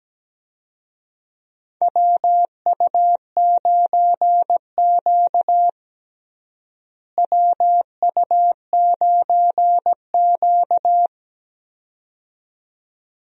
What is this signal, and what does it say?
Morse code, heard as a keyed tone, about 17 wpm, 700 Hz. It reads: WU9Q WU9Q
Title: USA Call Signs - 4 Characters - Repeated 2x - 17wpm